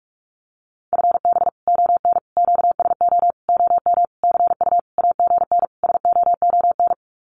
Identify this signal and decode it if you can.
Morse code, heard as a keyed tone, about 44 wpm, 700 Hz. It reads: FB ON QSO OM CU AGN SOON
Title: Common QSO Phrases 45wpm